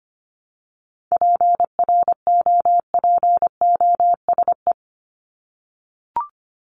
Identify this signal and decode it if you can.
Morse code, heard as a keyed tone, about 25 wpm, 700 Hz. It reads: PROPOSE
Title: ED Suffix - Top 100 Words with Base Words 25wpm